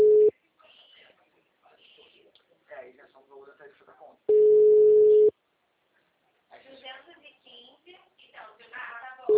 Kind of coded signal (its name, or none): telephone call progress tone